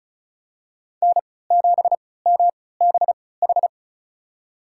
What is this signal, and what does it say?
Morse code, heard as a keyed tone, about 34 wpm, spaced as for 20 wpm, 700 Hz. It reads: N7MBH
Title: USA Call Signs - 3x Character Spacing - 35wpm